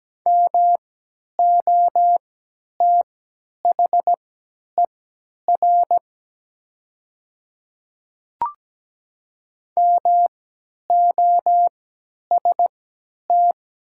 Morse code, heard as a keyed tone, about 17 wpm, spaced as for 10 wpm, 690 Hz. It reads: MOTHER MOST
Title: 3x Character Spacing - Top 300 Words - 17wpm